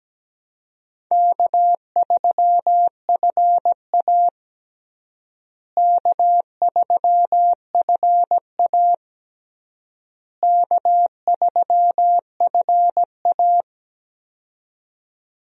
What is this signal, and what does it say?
Morse code, heard as a keyed tone, about 17 wpm, 700 Hz. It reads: K3FA K3FA K3FA
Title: USA Call Signs - 4 Characters - Repeated 3x - 17wpm